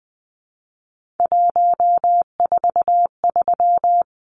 Morse code, heard as a keyed tone, about 20 wpm, 700 Hz. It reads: 143
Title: Numbers - 3-Digit 20wpm